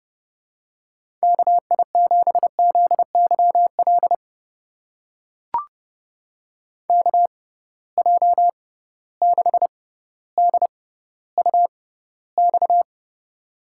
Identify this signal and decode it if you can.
Morse code, heard as a keyed tone, about 30 wpm, spaced as for 15 wpm, 700 Hz. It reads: KI7ZYL KJ6DUX
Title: USA Call Signs - 6x Character Spacing - 30wpm